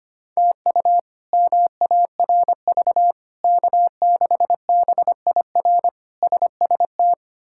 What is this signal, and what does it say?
Morse code, heard as a keyed tone, about 25 wpm, 700 Hz. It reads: TU MARV K6BIR SST
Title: K1USN SST QSO Practice - 25wpm